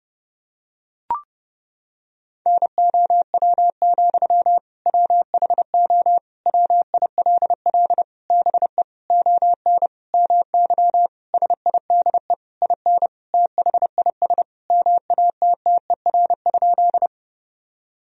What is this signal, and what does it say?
Morse code, heard as a keyed tone, about 29 wpm, 700 Hz. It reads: NOW, WHO WILL BE ON MY SIDE IN THIS MATTER?
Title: Sentences from Top 1000 Words - Review Part 1 - 30wpm